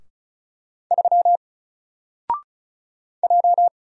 Morse code, heard as a keyed tone, about 36 wpm, 700 Hz. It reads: 3 J
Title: Single Letter-Number Alt 35wpm